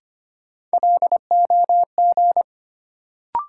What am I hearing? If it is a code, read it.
Morse code, heard as a keyed tone, about 25 wpm, 700 Hz. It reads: LOG
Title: Sentences from Top 1400 Words - New Word Drill Alt - 25wpm